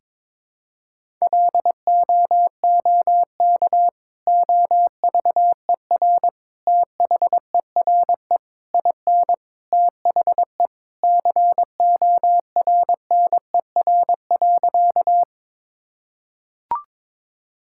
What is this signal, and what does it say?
Morse code, heard as a keyed tone, about 22 wpm, 700 Hz. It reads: LOOK OVER THERE IN THE CORNER.
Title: Sentences from Top 1600 Words 22wpm